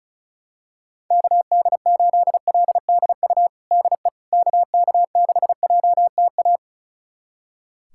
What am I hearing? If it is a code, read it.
Morse code, heard as a keyed tone, about 36 wpm, 700 Hz. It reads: KD8LDU DE KK6JTA
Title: USA Callsign DE Callsign - 6 Characters - 35wpm